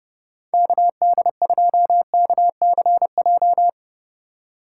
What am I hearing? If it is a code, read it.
Morse code, heard as a keyed tone, about 32 wpm, 700 Hz. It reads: KD2KCJ
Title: USA Call Signs - 6 Characters - 4x Character Spacing - 30wpm